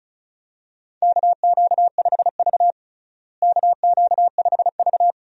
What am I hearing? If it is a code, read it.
Morse code, heard as a keyed tone, about 34 wpm, 700 Hz. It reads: KQ5V KQ5V
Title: USA Call Signs - 4 Characters - Repeated 2x - 35wpm